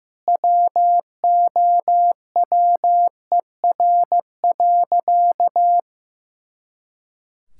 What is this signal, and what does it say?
Morse code, heard as a keyed tone, about 15 wpm, 700 Hz. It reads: WOWER.
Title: Pangram Challenge v2 - 15wpm